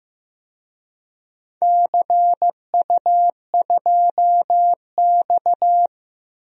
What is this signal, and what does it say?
Morse code, heard as a keyed tone, about 15 wpm, 700 Hz. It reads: CU2X
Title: DXCC-100 Call Signs - 4 Characters - Repeated 3x - 15wpm